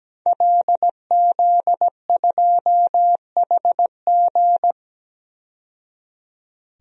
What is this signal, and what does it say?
Morse code, heard as a keyed tone, about 17 wpm, 700 Hz. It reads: LZ2HG
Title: DXCC-100 Call Signs - 5 Characters - 17wpm